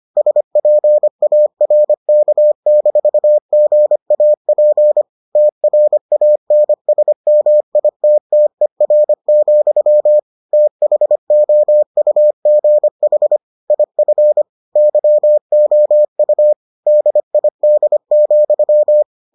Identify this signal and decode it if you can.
Morse code, heard as a keyed tone, about 25 wpm, 600 Hz. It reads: SPARK-GAP TRANSMITTER, THOUGH IF YOU DID,